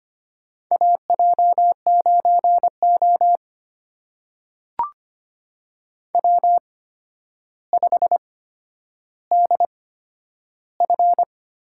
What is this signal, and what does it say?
Morse code, heard as a keyed tone, about 25 wpm, spaced as for 11 wpm, 700 Hz. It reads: AJ9O W5DF
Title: USA Call Signs - 8x Character Spacing - 25wpm